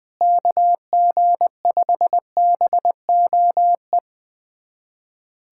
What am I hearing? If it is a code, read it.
Morse code, heard as a keyed tone, about 20 wpm, 700 Hz. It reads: KG5BOE
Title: USA Callsign DE Callsign - 6 Characters - 20wpm